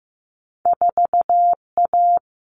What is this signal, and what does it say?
Morse code, heard as a keyed tone, about 15 wpm, 700 Hz. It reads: 4A
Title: All Cut Numbers - 2-Digit 15wpm